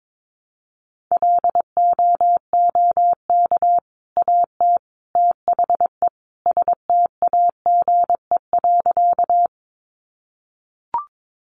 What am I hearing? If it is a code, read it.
Morse code, heard as a keyed tone, about 22 wpm, 700 Hz. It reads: LOOK AT THE STAGE.
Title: Sentences from Top 1200 Words 22wpm